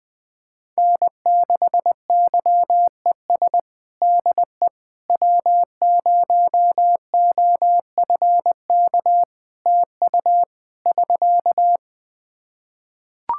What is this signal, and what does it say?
Morse code, heard as a keyed tone, about 21 wpm, 700 Hz. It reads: N6YES DE W0OFK TU <SK>
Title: K1USN SST QSO Practice 20wpm